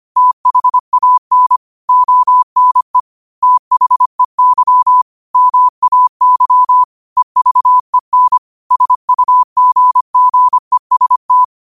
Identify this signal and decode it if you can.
Morse code, heard as a keyed tone, about 25 wpm, 1000 Hz. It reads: THAN ONE THEY MAY EVEN SUGGEST
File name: qotdfiles / 25wpm